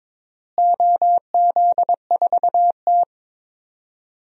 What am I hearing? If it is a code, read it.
Morse code, heard as a keyed tone, about 22 wpm, 700 Hz. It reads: OZ4T
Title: DXCC-100 Call Signs - 4 Characters - Repeated 2x - 22wpm